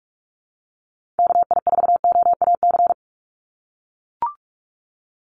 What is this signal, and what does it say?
Morse code, heard as a keyed tone, about 44 wpm, 700 Hz. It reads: KI4OAC
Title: USA Call Signs 45wpm